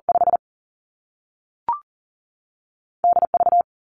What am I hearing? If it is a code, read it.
Morse code, heard as a keyed tone, about 40 wpm, 700 Hz. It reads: 5 DV
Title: Two Challenging Letter-Number Combinations 40wpm